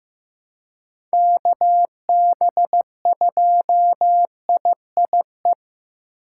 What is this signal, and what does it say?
Morse code, heard as a keyed tone, about 15 wpm, 700 Hz. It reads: KB2IIE
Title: USA Call Signs - 6 Characters - 3x Character Spacing - 15wpm